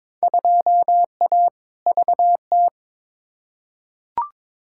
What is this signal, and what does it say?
Morse code, heard as a keyed tone, about 22 wpm, 700 Hz. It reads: 2A VT
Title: Field Day Exchanges Encore - 22wpm